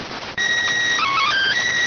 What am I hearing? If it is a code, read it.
SSTV (slow-scan television), PD 290.